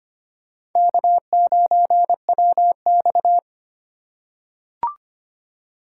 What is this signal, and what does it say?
Morse code, heard as a keyed tone, about 25 wpm, 700 Hz. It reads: K9WX
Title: USA Call Signs - 4 Characters - 4x Character Spacing - 25wpm